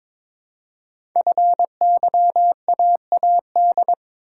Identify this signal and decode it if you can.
Morse code, heard as a keyed tone, about 22 wpm, 700 Hz. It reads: FYAAD
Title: Code Groups - 5 Letters - 22wpm